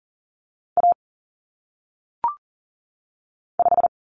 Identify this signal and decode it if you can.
Morse code, heard as a keyed tone, about 40 wpm, 700 Hz. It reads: A 5
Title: Common Cut Numbers - 1-Digit 40wpm